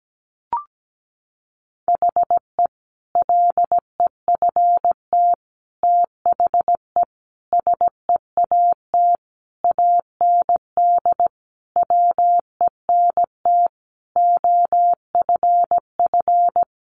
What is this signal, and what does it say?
Morse code, heard as a keyed tone, about 17 wpm, 700 Hz. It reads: HE LEFT THE SEAT AND WENT OFF
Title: Sentences from Top 1500 Words 17wpm